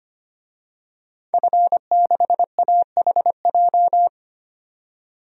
Morse code, heard as a keyed tone, about 25 wpm, 700 Hz. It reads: F6AHJ
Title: DXCC-100 Call Signs - 5 Characters - 25wpm